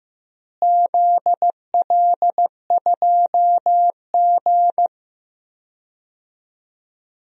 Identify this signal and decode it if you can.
Morse code, heard as a keyed tone, about 15 wpm, 700 Hz. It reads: ZL2G
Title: DXCC-100 Call Signs - 4 Characters - Repeated 3x - 15wpm